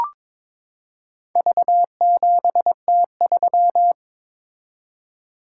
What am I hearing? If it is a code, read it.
Morse code, heard as a keyed tone, about 23 wpm, 700 Hz. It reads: V7T3
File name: practice / 22wpm / Four Letter-Number Combinations - 22wpm